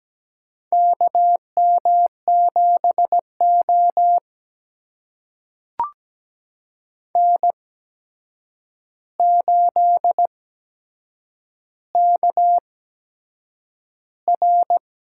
Morse code, heard as a keyed tone, about 17 wpm, spaced as for 7 wpm, 700 Hz. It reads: KM7O N8KR
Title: USA Call Signs - 4 Characters - 8x Character Spacing - 17wpm